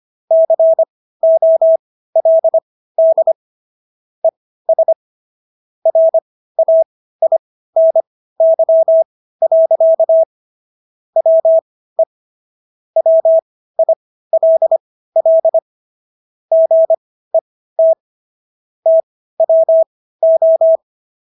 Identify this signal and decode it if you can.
Morse code, heard as a keyed tone, about 25 wpm, spaced as for 15 wpm, 650 Hz. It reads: COLD ES RAINY. WE WILL GET TWO